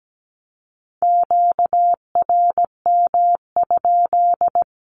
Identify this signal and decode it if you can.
Morse code, heard as a keyed tone, about 17 wpm, 700 Hz. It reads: QRM?
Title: Q-Codes with Spoken Meaning - 17wpm